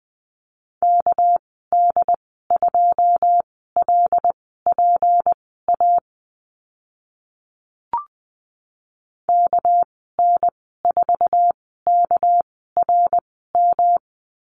Morse code, heard as a keyed tone, about 20 wpm, spaced as for 14 wpm, 700 Hz. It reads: KD2LPA KN4KRM
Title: 213 Lesson 34g - Number 8 - Call Signs 20wpm